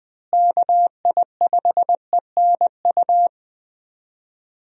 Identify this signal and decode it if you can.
Morse code, heard as a keyed tone, about 20 wpm, 700 Hz. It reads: KI5ENU
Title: USA Call Signs - 6 Characters - Repeated 2x - 20wpm